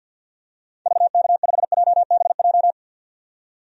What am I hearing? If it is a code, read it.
Morse code, heard as a keyed tone, about 52 wpm, 700 Hz. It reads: VK5JBJ
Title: DXCC-100 Call Signs - 6 Characters - 50wpm